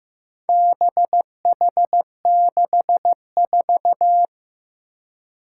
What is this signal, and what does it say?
Morse code, heard as a keyed tone, about 15 wpm, 700 Hz. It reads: BH64
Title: Four Challenging Letter-Number Combinations 15wpm